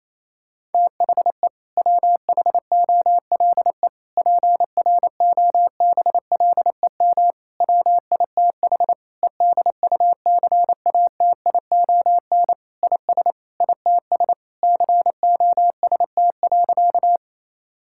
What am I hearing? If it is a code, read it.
Morse code, heard as a keyed tone, about 28 wpm, 700 Hz. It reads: THE WHOLE PROBLEM WITH EDUCATION IS ITS COST.
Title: Sentences from Top 1000 Words - Review Part 6 - 28wpm